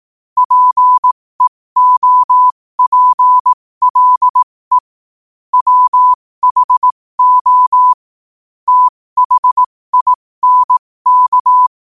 Morse code, heard as a keyed tone, about 18 wpm, spaced as for 15 wpm, 1000 Hz. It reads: PEOPLE WHO THINK
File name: qotdfiles / 15wpm